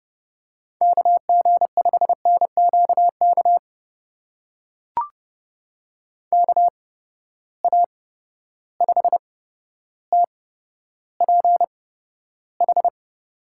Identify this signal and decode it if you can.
Morse code, heard as a keyed tone, about 32 wpm, spaced as for 13 wpm, 700 Hz. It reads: KG5NQK KA5TPH
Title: USA Call Signs - 6 Characters - 8x Character Spacing - 30wpm